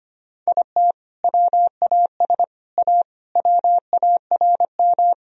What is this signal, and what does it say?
Morse code, heard as a keyed tone, about 25 wpm, 700 Hz. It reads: IT WAS A WARM
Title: Sentences from Top 1000 Words - Review Part 2 - 25wpm